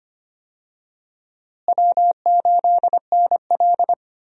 Morse code, heard as a keyed tone, about 25 wpm, 700 Hz. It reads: W8NL
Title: USA Call Signs - 4 Characters - Repeated 3x - 25wpm